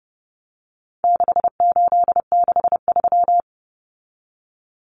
Morse code, heard as a keyed tone, about 30 wpm, 700 Hz. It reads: 6863
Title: Common Cut Numbers - 4-Digit 30wpm